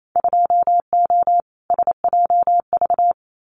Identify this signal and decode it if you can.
Morse code, heard as a keyed tone, about 28 wpm, 700 Hz. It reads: 2O SJV
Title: Winter Field Day Exchanges Encore 28wpm